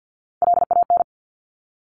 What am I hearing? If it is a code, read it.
Morse code, heard as a keyed tone, about 50 wpm, 700 Hz. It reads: LAN
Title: Top Three Letter Combinations in Words 50wpm